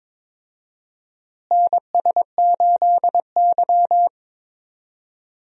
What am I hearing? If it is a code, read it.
Morse code, heard as a keyed tone, about 22 wpm, 700 Hz. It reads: NS8Y
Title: USA Call Signs - 4 Characters - Repeated 2x - 22wpm